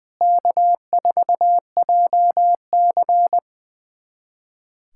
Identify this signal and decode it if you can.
Morse code, heard as a keyed tone, about 20 wpm, 700 Hz. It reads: K4JC
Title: USA Callsign DE Callsign - 4 Characters - 20wpm